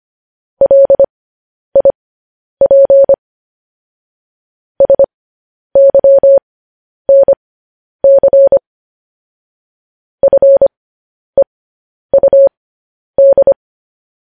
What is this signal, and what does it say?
Morse code, heard as a keyed tone, about 25 wpm, spaced as for 10 wpm, 550 Hz. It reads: LIP SYNC FEUD
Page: news headlines in morse code at 10 wpm